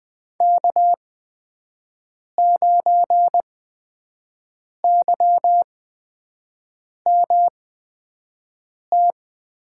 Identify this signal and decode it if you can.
Morse code, heard as a keyed tone, about 20 wpm, spaced as for 5 wpm, 700 Hz. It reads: K9YMT
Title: USA Call Signs - 5 Characters - 8x Character Spacing - 20wpm